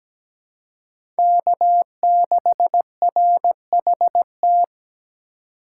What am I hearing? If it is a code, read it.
Morse code, heard as a keyed tone, about 17 wpm, 700 Hz. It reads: K6RHT
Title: USA Call Signs - 2x Character Spacing - 17wpm